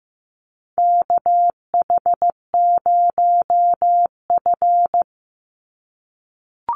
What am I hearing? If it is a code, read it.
Morse code, heard as a keyed tone, about 15 wpm, 700 Hz. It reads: KH0F
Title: USA Call Signs 15wpm